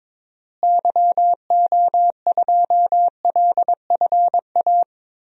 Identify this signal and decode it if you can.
Morse code, heard as a keyed tone, about 22 wpm, 700 Hz. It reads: YO2LFA
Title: DXCC-100 Call Signs - 6 Characters - Repeated 2x - 22wpm